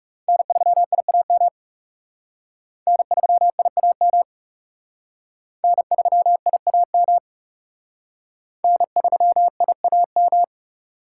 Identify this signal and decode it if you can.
Morse code, heard as a keyed tone, about 37 wpm, 700 Hz. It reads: N3IAM N3IAM N3IAM N3IAM
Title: Speed Racing - USA 1x3 Call Signs 30wpm